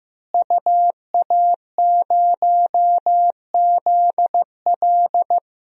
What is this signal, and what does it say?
Morse code, heard as a keyed tone, about 15 wpm, 700 Hz. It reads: UA0ZL
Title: DXCC-100 Call Signs - 5 Characters - Repeated 3x - 15wpm